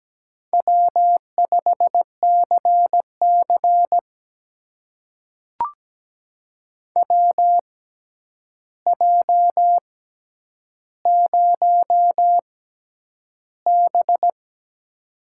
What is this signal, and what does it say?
Morse code, heard as a keyed tone, about 17 wpm, spaced as for 9 wpm, 700 Hz. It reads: W5CC WJ0B
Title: USA Call Signs - 6x Character Spacing - 17wpm